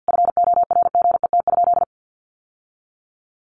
Morse code, heard as a keyed tone, about 50 wpm, 700 Hz. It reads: FORGET?
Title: Sentences from Top 1300 Words 50wpm